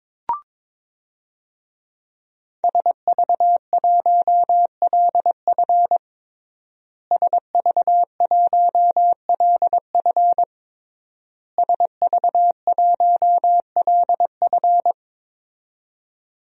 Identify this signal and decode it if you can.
Morse code, heard as a keyed tone, about 22 wpm, 700 Hz. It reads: SV1LF SV1LF SV1LF
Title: DXCC-100 Call Signs - 5 Characters - Repeated 3x - 22wpm